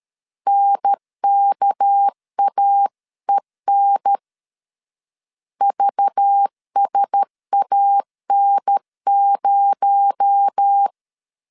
Morse code, heard as a keyed tone, about 13 wpm, 800 Hz. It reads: NKAEN VSAN0
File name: CW / DET / 1 6 1500C S B 12wpm 3431